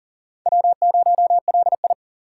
Morse code, heard as a keyed tone, about 40 wpm, 700 Hz. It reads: W0LI
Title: USA Call Signs - 4 Characters - 4x Character Spacing - 40wpm